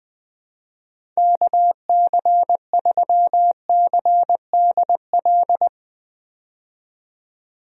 Morse code, heard as a keyed tone, about 20 wpm, 700 Hz. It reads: KC3CDL
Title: USA Call Signs - 6 Characters - 20wpm